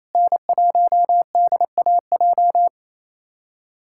Morse code, heard as a keyed tone, about 28 wpm, 700 Hz. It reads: N1DAJ
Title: USA Callsign DE Callsign 28wpm